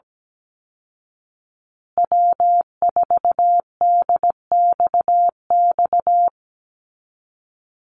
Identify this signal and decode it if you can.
Morse code, heard as a keyed tone, about 17 wpm, 700 Hz. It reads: W4DXX
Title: Call Signs Repeated 2x - 17wpm